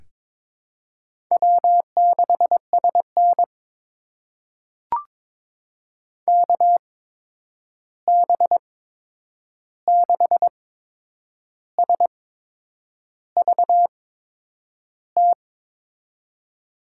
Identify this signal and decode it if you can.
Morse code, heard as a keyed tone, about 22 wpm, spaced as for 8 wpm, 700 Hz. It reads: W6SN KB6SVT
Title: USA Call Signs - 8x Character Spacing - 22wpm